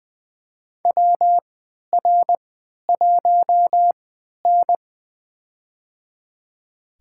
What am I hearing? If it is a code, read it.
Morse code, heard as a keyed tone, about 20 wpm, spaced as for 11 wpm, 700 Hz. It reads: WR1N
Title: USA Call Signs - 4 Characters - 3x Character Spacing - 20wpm